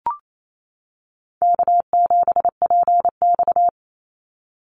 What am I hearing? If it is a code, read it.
Morse code, heard as a keyed tone, about 28 wpm, 700 Hz. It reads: K7PX
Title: NA SOTA Activator Callsigns 28wpm